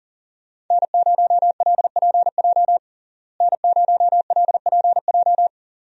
Morse code, heard as a keyed tone, about 40 wpm, 700 Hz. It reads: N0LPJ N0LPJ
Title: USA Call Signs - 5 Characters - Repeated 2x - 40wpm